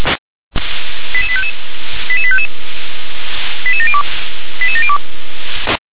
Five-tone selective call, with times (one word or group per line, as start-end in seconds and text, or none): 1.1-1.5 88855
2.1-2.5 88855
3.7-4.0 88702
4.6-5.0 88702